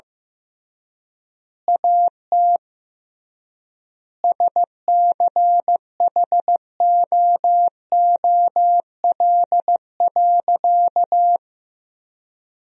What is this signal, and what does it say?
Morse code, heard as a keyed tone, about 15 wpm, 700 Hz. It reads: AT SCHOOL.
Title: 3x Word-Spacing - Sentences from Top 300 Words - 15wpm